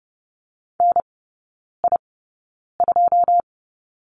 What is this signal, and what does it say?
Morse code, heard as a keyed tone, about 30 wpm, spaced as for 9 wpm, 700 Hz. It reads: NI2
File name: 081 Lesson 14e - Number 2 - Three Characters 30wpm